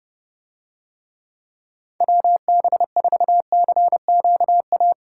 Morse code, heard as a keyed tone, about 28 wpm, 700 Hz. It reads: WB4CQA